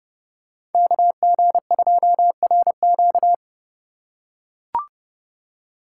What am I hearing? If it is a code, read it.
Morse code, heard as a keyed tone, about 28 wpm, 700 Hz. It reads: KG2RQ